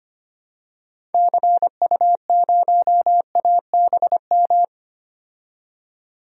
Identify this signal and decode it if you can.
Morse code, heard as a keyed tone, about 25 wpm, 690 Hz. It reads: CU0ABM
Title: DXCC-100 Call Signs - 6 Characters - 25wpm